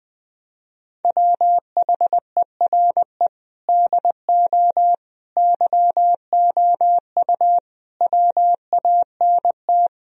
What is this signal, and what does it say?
Morse code, heard as a keyed tone, about 21 wpm, 700 Hz. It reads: WHERE DO YOU WANT
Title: Sentences from Top 1000 Words - Review Part 2 - 20wpm